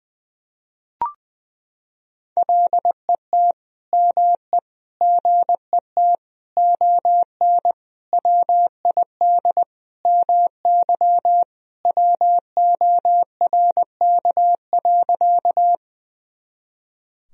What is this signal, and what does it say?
Morse code, heard as a keyed tone, about 20 wpm, 700 Hz. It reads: LET ME GET ON WID MY WORK.
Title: Sentences from Top 300 Words and Abbreviations 20wpm